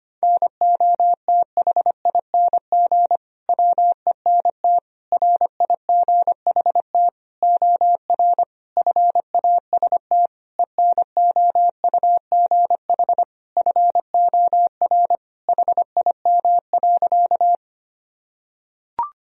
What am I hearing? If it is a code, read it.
Morse code, heard as a keyed tone, about 25 wpm, 700 Hz. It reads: NOTHING WENT RIGHT OR FAST ENOUGH FOR HIM.
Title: Sentences from Top 1000 Words - Review Part 2 - 25wpm